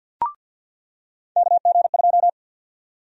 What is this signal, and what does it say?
Morse code, heard as a keyed tone, about 53 wpm, 700 Hz. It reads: KG2
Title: USA Call Sign Prefixes - 50wpm